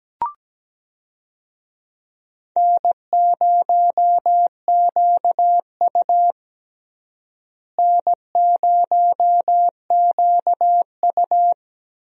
Morse code, heard as a keyed tone, about 17 wpm, 700 Hz. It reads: N0QU N0QU
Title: USA Call Signs - 4 Characters - Repeated 2x - 17wpm